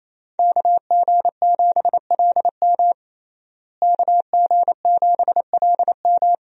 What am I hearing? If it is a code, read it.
Morse code, heard as a keyed tone, about 28 wpm, 700 Hz. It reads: KG7LM KG7LM
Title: USA Call Signs - 5 Characters - Repeated 2x - 28wpm